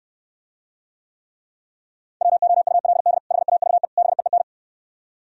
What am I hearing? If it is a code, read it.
Morse code, heard as a keyed tone, about 67 wpm, 700 Hz. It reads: WOULD HAVE BEEN